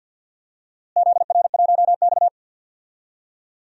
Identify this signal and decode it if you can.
Morse code, heard as a keyed tone, about 47 wpm, 700 Hz. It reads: ZR1X
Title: DXCC-100 Call Signs - 4 Characters - Repeated 3x - 50wpm